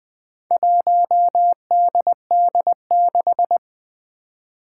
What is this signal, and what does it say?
Morse code, heard as a keyed tone, about 20 wpm, 700 Hz. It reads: 1DD6
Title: Four Challenging Letter-Number Combinations 20wpm